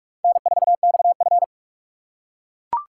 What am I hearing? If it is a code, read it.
Morse code, heard as a keyed tone, about 37 wpm, 700 Hz. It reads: N4XF
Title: Speed Racing - USA 1x2 Call Signs 30wpm